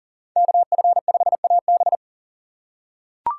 Morse code, heard as a keyed tone, about 40 wpm, 700 Hz. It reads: KF5AB
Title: USA Call Signs - 5 Characters - 3x Character Spacing - 40wpm